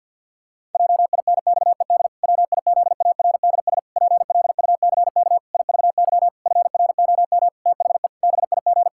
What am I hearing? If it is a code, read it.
Morse code, heard as a keyed tone, about 50 wpm, 700 Hz. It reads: JINXED WIZARDS PLUCK IVY FROM THE BIG
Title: Pangram Challenge v2 - 50wpm